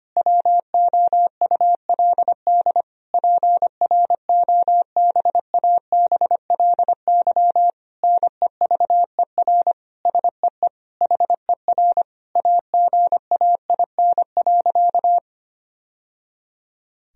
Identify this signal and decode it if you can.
Morse code, heard as a keyed tone, about 25 wpm, 700 Hz. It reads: WOULD PROBABLY NEVER SEE HER AGAIN.
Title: Sentences from Top 1000 Words - Review Part 4 - 25wpm